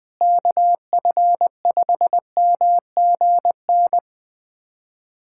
Morse code, heard as a keyed tone, about 20 wpm, 700 Hz. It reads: KF5MGN
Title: USA Callsign DE Callsign - 6 Characters - 20wpm